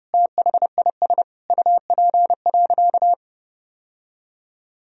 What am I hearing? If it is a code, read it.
Morse code, heard as a keyed tone, about 30 wpm, 700 Hz. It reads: THIS UP.
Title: Sentences from Top 1000 Words - Review Part 1 - 30wpm